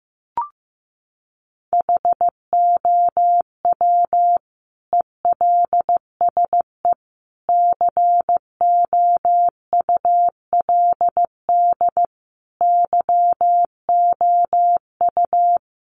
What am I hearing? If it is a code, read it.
Morse code, heard as a keyed tone, about 15 wpm, 700 Hz. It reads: HOW ELSE COULD YOU
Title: Sentences from Top 1300 Words 15wpm